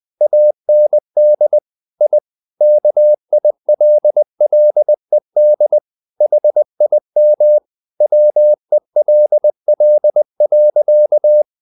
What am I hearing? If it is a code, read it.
Morse code, heard as a keyed tone, about 20 wpm, 600 Hz. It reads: AND I KILLED HIM WELL.
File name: the old man and the sea part 2 20wpm